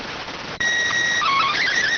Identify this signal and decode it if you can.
SSTV (slow-scan television), PD 290.